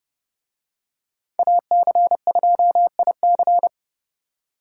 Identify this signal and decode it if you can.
Morse code, heard as a keyed tone, about 30 wpm, 700 Hz. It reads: AC2IC